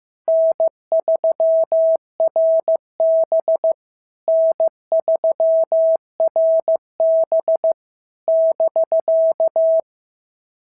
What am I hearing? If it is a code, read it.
Morse code, heard as a keyed tone, about 14 wpm, 650 Hz. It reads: N3RB N3RB <BK>
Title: SOTA QSO Practice 15wpm